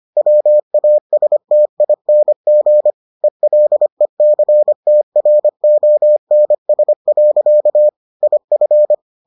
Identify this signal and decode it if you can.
Morse code, heard as a keyed tone, about 25 wpm, 600 Hz. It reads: WASTING ELECTRONS. IF